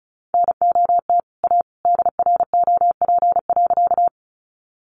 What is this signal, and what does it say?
Morse code, heard as a keyed tone, about 34 wpm, 700 Hz. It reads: NOT A DROP.
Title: Sentences from Top 1400 Words 35wpm